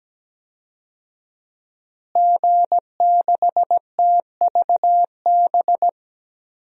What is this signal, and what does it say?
Morse code, heard as a keyed tone, about 17 wpm, 700 Hz. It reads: G6TVB
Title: DXCC-100 Call Signs - 5 Characters - Repeated 3x - 17wpm